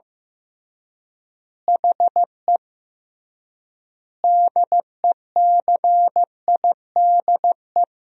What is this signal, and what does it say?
Morse code, heard as a keyed tone, about 15 wpm, 700 Hz. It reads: HE DECIDE
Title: 3x Word-Spacing - Sentences from Top 500 Words - 15wpm